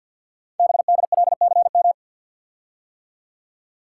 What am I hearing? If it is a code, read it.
Morse code, heard as a keyed tone, about 52 wpm, 700 Hz. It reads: BDLCM
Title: Code Groups - 5 Letters - 50wpm